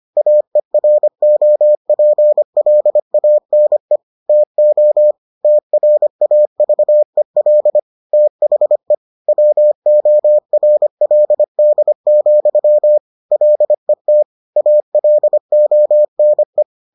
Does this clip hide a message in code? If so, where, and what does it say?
Yes: Morse code from 0.2 to 16.6 s, heard as a keyed tone, about 25 wpm, 600 Hz. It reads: AEROPLANE TO TRAVEL THE WORLD, LET ALONE